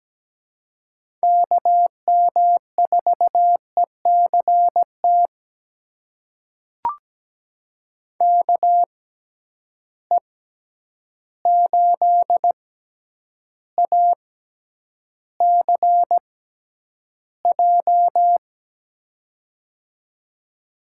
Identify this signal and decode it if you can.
Morse code, heard as a keyed tone, about 17 wpm, spaced as for 9 wpm, 700 Hz. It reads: KM4ECT KE8ACJ
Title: USA Call Signs - 6x Character Spacing - 17wpm